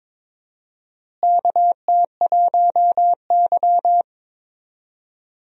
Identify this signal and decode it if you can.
Morse code, heard as a keyed tone, about 22 wpm, 700 Hz. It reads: KT1Y
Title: USA Call Signs - 4 Characters - Repeated 2x - 22wpm